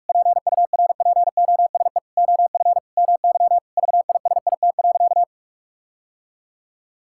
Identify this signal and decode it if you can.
Morse code, heard as a keyed tone, about 46 wpm, 700 Hz. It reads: PURPOSE OF MY VISIT.
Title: Sentences from Top 1000 Words - Review Part 6 - 45wpm